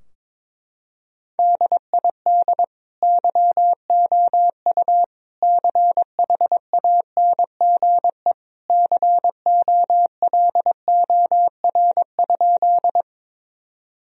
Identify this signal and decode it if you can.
Morse code, heard as a keyed tone, about 21 wpm, 700 Hz. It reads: DID YOU CHANGE COLOR?
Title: Sentences from Top 1000 Words - Review Part 2 - 22wpm